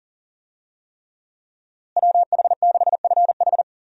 Morse code, heard as a keyed tone, about 40 wpm, 700 Hz. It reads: WH6FH
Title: USA Call Signs - 5 Characters - Repeated 3x - 40wpm